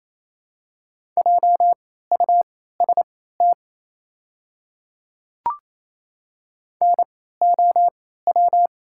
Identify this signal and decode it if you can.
Morse code, heard as a keyed tone, about 28 wpm, spaced as for 16 wpm, 700 Hz. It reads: JUST NOW
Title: 3x Character Spacing - Top 300 Words - 28wpm